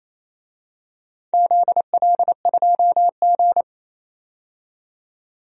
Morse code, heard as a keyed tone, about 27 wpm, 700 Hz. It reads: ZL2G
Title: DXCC-100 Call Signs - 4 Characters - 28wpm